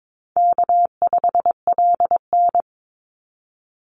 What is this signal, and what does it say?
Morse code, heard as a keyed tone, about 22 wpm, 700 Hz. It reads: K5LN
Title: Easy Call Signs - 22wpm